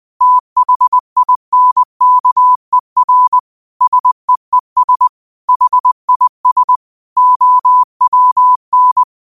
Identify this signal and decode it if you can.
Morse code, heard as a keyed tone, about 20 wpm, 1000 Hz. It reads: THINKER SEES HIS OWN
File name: qotdfiles / 20wpm